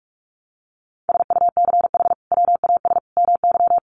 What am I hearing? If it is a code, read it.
Morse code, heard as a keyed tone, about 44 wpm, 700 Hz. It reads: SUCH WAS MY
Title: Sentences from Top 800 Words 45wpm